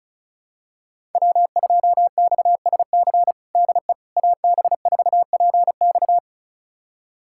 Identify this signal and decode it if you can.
Morse code, heard as a keyed tone, about 36 wpm, 700 Hz. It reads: W2XSC DE AB4PX
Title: USA Callsign DE Callsign - 5 Characters - 35wpm